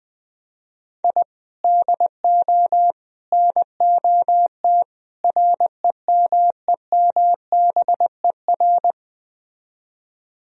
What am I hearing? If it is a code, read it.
Morse code, heard as a keyed tone, about 20 wpm, 700 Hz. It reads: I DO NOT REMEMBER
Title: Speed-Racing - Sets of 4 Words 17wpm